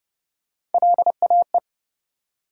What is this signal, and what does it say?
Morse code, heard as a keyed tone, about 29 wpm, 700 Hz. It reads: LAE
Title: USA Call Sign Postfixes - 3 Letters 30wpm